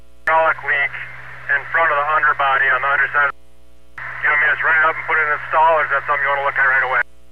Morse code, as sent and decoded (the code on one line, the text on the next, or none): none